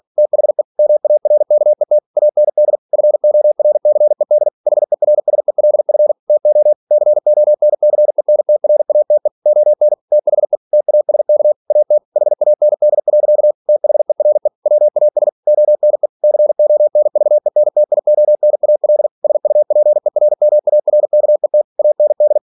Morse code, heard as a keyed tone, about 46 wpm, 600 Hz. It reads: THE MARKET AND FORCED HERSELF TO CONCENTRATE ON THE TASK AT HAND. THERE WAS ONE CONVENTIONAL SUPERMARKET AND
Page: Never 47wpm